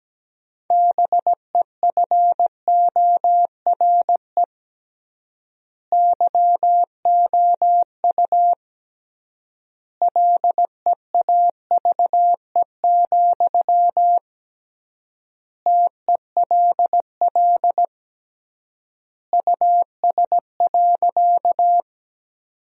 Morse code, heard as a keyed tone, about 17 wpm, 700 Hz. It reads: BEFORE YOU LEAVE, TELL US.
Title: 3x Word-Spacing - Sentences from Top 400 Words - 17wpm